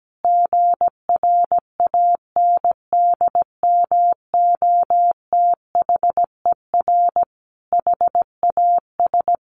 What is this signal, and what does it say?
Morse code, heard as a keyed tone, about 17 wpm, 700 Hz. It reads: GRANDMOTHER HAS
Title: Sentences from Top 1200 Words 17wpm